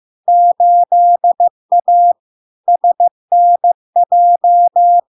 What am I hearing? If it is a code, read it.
Morse code, heard as a keyed tone, about 15 wpm, 700 Hz. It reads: 8A SNJ
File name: LARK-FD-2022-Log-in-CW-15wpm-700-Hz